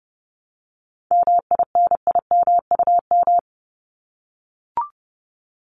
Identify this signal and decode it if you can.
Morse code, heard as a keyed tone, about 30 wpm, 700 Hz. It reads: MINIMUM
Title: Top 1251-1500 Words 30wpm